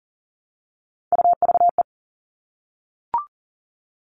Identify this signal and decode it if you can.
Morse code, heard as a keyed tone, about 40 wpm, 700 Hz. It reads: UVE
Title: All Cut Numbers - 3-Digit 40wpm